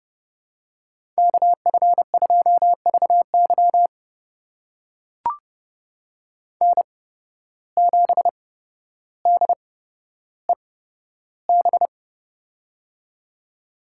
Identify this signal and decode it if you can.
Morse code, heard as a keyed tone, about 30 wpm, spaced as for 13 wpm, 700 Hz. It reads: KF2VY N7DEB
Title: USA Call Signs - 5 Characters - 8x Character Spacing - 30wpm